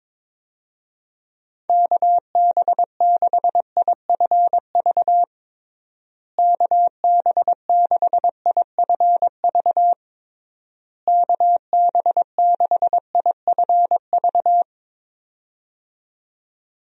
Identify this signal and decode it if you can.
Morse code, heard as a keyed tone, about 22 wpm, 700 Hz. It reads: KB6IFV KB6IFV KB6IFV
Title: USA Call Signs - 6 Characters - Repeated 3x - 22wpm